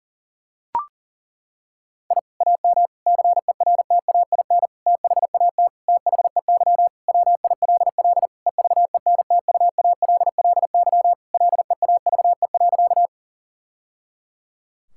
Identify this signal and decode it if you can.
Morse code, heard as a keyed tone, about 37 wpm, 700 Hz. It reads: I AM CERTAIN THAT THEY WILL EVENTUALLY LEAVE.
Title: Sentences from Top 1000 Words - Review Part 6 - 40wpm